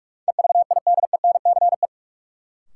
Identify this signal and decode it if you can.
Morse code, heard as a keyed tone, about 47 wpm, 700 Hz. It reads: EVIDENCE
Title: Sentences from Top 1300 Words - New Word Drill - 45wpm